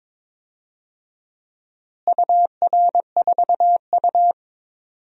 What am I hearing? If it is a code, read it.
Morse code, heard as a keyed tone, about 22 wpm, 700 Hz. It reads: UR4U
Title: DXCC-100 Call Signs - 4 Characters - Repeated 2x - 22wpm